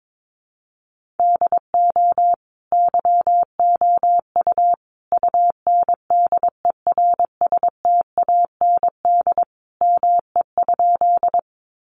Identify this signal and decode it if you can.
Morse code, heard as a keyed tone, about 22 wpm, 700 Hz. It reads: DO YOU UNDERSTAND ME?
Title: Sentences from Top 600 Words 22wpm